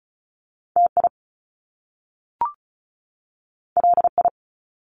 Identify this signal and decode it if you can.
Morse code, heard as a keyed tone, about 36 wpm, 700 Hz. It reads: T I L I
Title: Top Two Letter Combinations in Words 35wpm